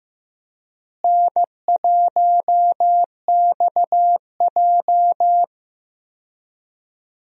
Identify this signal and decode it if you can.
Morse code, heard as a keyed tone, about 15 wpm, 700 Hz. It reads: N1XJ
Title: USA Call Signs - 4 Characters - Repeated 2x - 15wpm